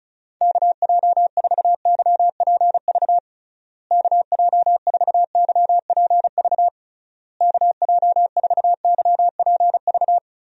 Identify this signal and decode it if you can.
Morse code, heard as a keyed tone, about 36 wpm, 700 Hz. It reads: KJ4YPV KJ4YPV KJ4YPV
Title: USA Call Signs - 6 Characters - Repeated 3x - 35wpm